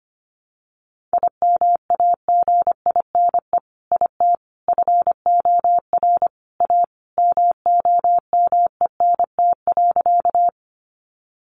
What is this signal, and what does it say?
Morse code, heard as a keyed tone, about 25 wpm, 700 Hz. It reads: IMAGINE IT FOR A MOMENT.
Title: Sentences from Top 1400 Words 25wpm